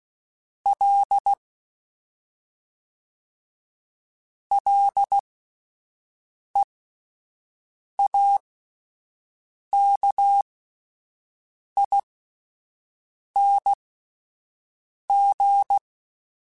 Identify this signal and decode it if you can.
Morse code, heard as a keyed tone, about 16 wpm, spaced as for 5 wpm, 800 Hz. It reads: L LEAKING